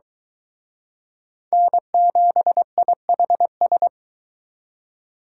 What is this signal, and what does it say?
Morse code, heard as a keyed tone, about 23 wpm, 700 Hz. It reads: N7IHS